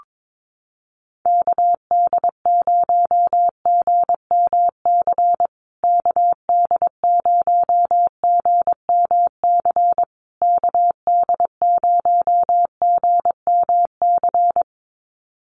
Morse code, heard as a keyed tone, about 22 wpm, 700 Hz. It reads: KD0GMC KD0GMC KD0GMC
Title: USA Call Signs Repeated 3x 22wpm